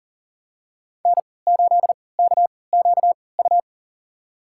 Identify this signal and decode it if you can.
Morse code, heard as a keyed tone, about 40 wpm, spaced as for 23 wpm, 700 Hz. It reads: N8KQU
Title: USA Call Signs - 5 Characters - 3x Character Spacing - 40wpm